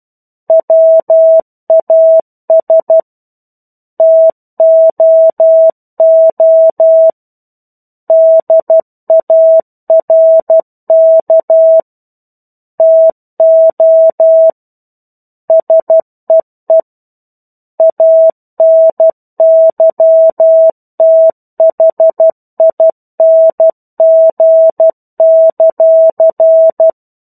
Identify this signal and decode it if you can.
Morse code, heard as a keyed tone, about 12 wpm, 650 Hz. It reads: WAS TOO DARK TO SEE ANYTHING;
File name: 01-Chapter-01 12 WPM